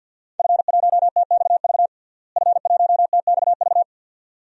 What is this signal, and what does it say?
Morse code, heard as a keyed tone, about 51 wpm, 700 Hz. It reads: F1TXV F1TXV
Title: DXCC-100 Call Signs - 5 Characters - Repeated 2x - 50wpm